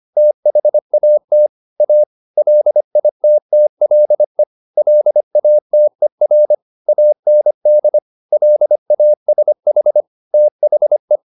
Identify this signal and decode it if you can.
Morse code, heard as a keyed tone, about 25 wpm, 600 Hz. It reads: THAT A LITTLE LATER AND LASH THE